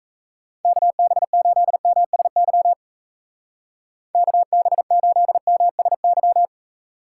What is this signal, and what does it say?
Morse code, heard as a keyed tone, about 34 wpm, 700 Hz. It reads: KB8MSY KB8MSY
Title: Speed Racing - USA Call Signs 28wpm